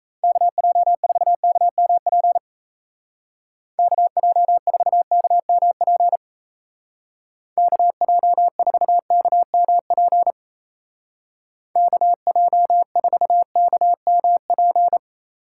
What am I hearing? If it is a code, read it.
Morse code, heard as a keyed tone, about 34 wpm, 700 Hz. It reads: KJ4KMP KJ4KMP KJ4KMP KJ4KMP
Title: Speed Racing - USA 2x3 Call Signs 28wpm